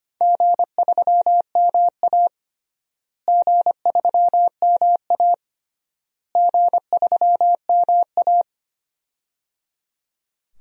Morse code, heard as a keyed tone, about 25 wpm, 700 Hz. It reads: G3MA G3MA G3MA
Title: DXCC-100 Call Signs - 4 Characters - Repeated 3x - 25wpm